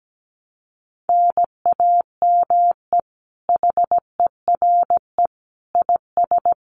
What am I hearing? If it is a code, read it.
Morse code, heard as a keyed tone, about 17 wpm, 700 Hz. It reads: NAME HERE IS
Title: Common QSO Phrases 17wpm